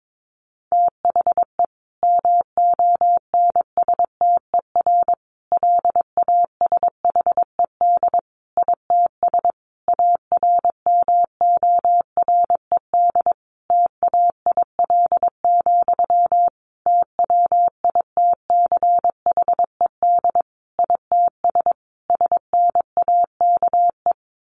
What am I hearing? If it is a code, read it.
Morse code, heard as a keyed tone, about 22 wpm, 700 Hz. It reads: THE MONSTER LASHED ITS ARMORED TAIL, TWITCHED ITS SNAKE
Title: A Sound Of Thunder 22wpm